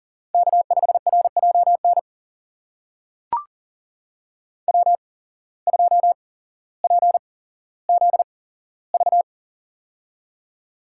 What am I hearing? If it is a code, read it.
Morse code, heard as a keyed tone, about 40 wpm, spaced as for 17 wpm, 700 Hz. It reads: K5RJN W2PZV
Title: USA Call Signs - 5 Characters - 8x Character Spacing - 40wpm